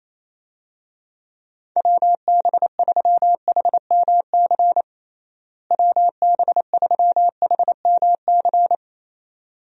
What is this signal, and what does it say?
Morse code, heard as a keyed tone, about 28 wpm, 700 Hz. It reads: WB3HMC WB3HMC
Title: USA Call Signs - 6 Characters - Repeated 2x - 28wpm